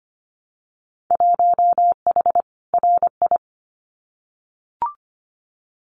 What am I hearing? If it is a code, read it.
Morse code, heard as a keyed tone, about 25 wpm, 700 Hz. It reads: 1H RI
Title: Winter Field Day Exchanges 25wpm